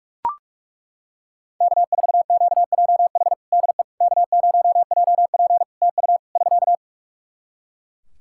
Morse code, heard as a keyed tone, about 41 wpm, 700 Hz. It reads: K4QJH DE K0JP TU <SK>
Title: Practice QSO - Exchange 4 - Ending - 45wpm